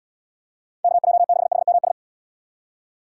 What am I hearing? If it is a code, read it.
Morse code, heard as a keyed tone, about 79 wpm, 700 Hz. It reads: KJ6HMS